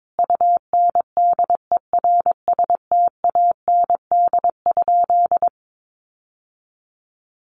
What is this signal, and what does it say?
Morse code, heard as a keyed tone, about 22 wpm, 700 Hz. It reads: UNDERSTAND?
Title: Sentences from Top 600 Words 22wpm